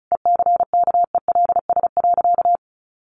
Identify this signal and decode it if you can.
Morse code, heard as a keyed tone, about 34 wpm, 700 Hz. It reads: ECKELS.
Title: A Sound Of Thunder 35wpm